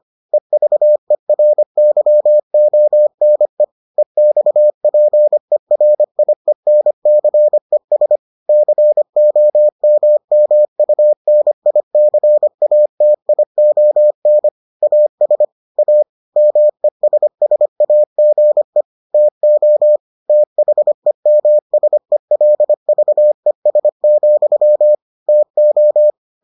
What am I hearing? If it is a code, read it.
Morse code, heard as a keyed tone, about 25 wpm, 600 Hz. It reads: EVERYONE EXPERIENCES COMMUNICATION AS A MESSAGE TO THEMSELVES, TO